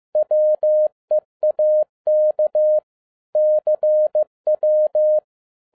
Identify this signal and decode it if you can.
Morse code, heard as a keyed tone, about 15 wpm, 610 Hz. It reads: WEAK CW
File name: EXCITING QRP CW OPERATION 15wpm